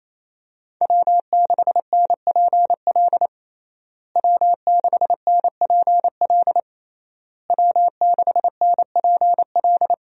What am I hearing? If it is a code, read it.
Morse code, heard as a keyed tone, about 28 wpm, 700 Hz. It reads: W6NPL W6NPL W6NPL
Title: USA Call Signs - 5 Characters - Repeated 3x - 28wpm